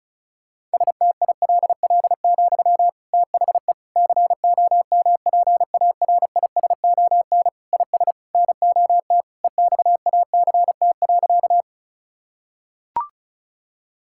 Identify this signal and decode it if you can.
Morse code, heard as a keyed tone, about 34 wpm, 700 Hz. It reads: STILL, THE COMPARISON IS NOT EXACT.
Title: Sentences from Top 1600 Words 35wpm